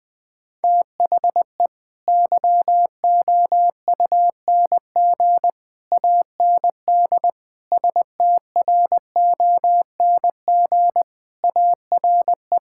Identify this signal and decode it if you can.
Morse code, heard as a keyed tone, about 20 wpm, 700 Hz. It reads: THE YOUNG AND STRONG ARE
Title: Sentences from Top 1000 Words - Review Part 5 - 20wpm